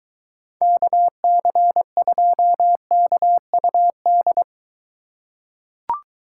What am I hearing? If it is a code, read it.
Morse code, heard as a keyed tone, about 23 wpm, 700 Hz. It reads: KC2KUD